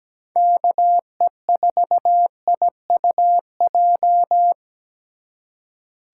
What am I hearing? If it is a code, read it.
Morse code, heard as a keyed tone, about 20 wpm, 700 Hz. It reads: KE4IUJ
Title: Speed Racing - USA 2x3 Call Signs 17wpm